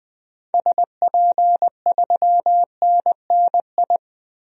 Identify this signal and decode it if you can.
Morse code, heard as a keyed tone, about 21 wpm, 700 Hz. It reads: SP3NNI
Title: DXCC-100 Call Signs - 6 Characters - 20wpm